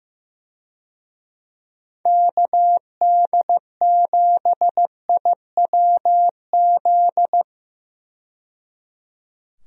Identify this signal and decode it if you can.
Morse code, heard as a keyed tone, about 15 wpm, 700 Hz. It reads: KD7IWZ
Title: USA Call Signs - 6 Characters - 15wpm